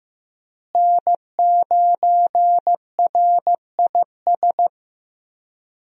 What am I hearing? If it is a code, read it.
Morse code, heard as a keyed tone, about 15 wpm, 700 Hz. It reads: N9RIS
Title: USA Call Signs - 5 Characters - Repeated 2x - 15wpm